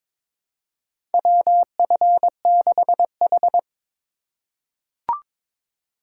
Morse code, heard as a keyed tone, about 22 wpm, 700 Hz. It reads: WF6H
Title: USA Call Signs - 4 Characters - 2x Character Spacing - 22wpm